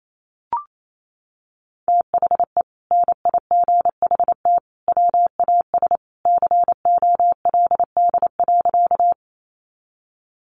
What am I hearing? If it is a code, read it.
Morse code, heard as a keyed tone, about 28 wpm, 700 Hz. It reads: THE NIGHT WAS COLD.
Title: Sentences from Top 400 Words 28wpm